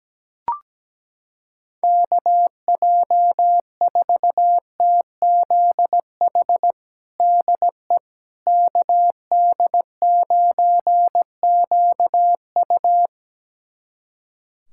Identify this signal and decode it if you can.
Morse code, heard as a keyed tone, about 17 wpm, 700 Hz. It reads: KJ4TZH DE KD9QU
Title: USA Callsign DE Callsign 17wpm